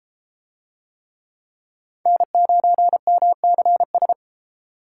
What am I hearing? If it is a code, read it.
Morse code, heard as a keyed tone, about 31 wpm, 700 Hz. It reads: N9MCS